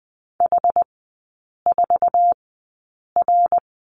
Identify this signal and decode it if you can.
Morse code, heard as a keyed tone, about 20 wpm, spaced as for 8 wpm, 700 Hz. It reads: H4R
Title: 133 Lesson 22e - Question Mark - Three Characters 20wpm